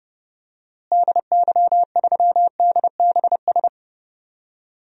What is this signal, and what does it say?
Morse code, heard as a keyed tone, about 30 wpm, 700 Hz. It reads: DY3DBS
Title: DXCC-100 Call Signs - 6 Characters - Repeated 2x - 30wpm